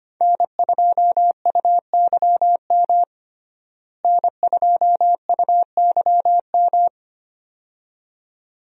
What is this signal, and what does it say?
Morse code, heard as a keyed tone, about 25 wpm, 700 Hz. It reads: N2UYM N2UYM
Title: USA Call Signs - 5 Characters - Repeated 2x - 25wpm